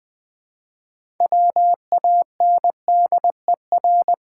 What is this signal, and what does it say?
Morse code, heard as a keyed tone, about 20 wpm, 700 Hz. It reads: WANDER
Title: 6 Letter Words - First 3 Letters Given - 20wpm